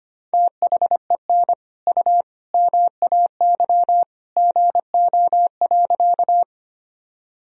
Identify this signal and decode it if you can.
Morse code, heard as a keyed tone, about 25 wpm, 700 Hz. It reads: THEN U MAY GO.
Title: Sentences from Top 100 Words and Abbreviations 25wpm